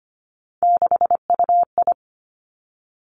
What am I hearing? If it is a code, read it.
Morse code, heard as a keyed tone, about 25 wpm, 700 Hz. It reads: 6UI
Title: Three Challenging Letter-Number Combinations 25wpm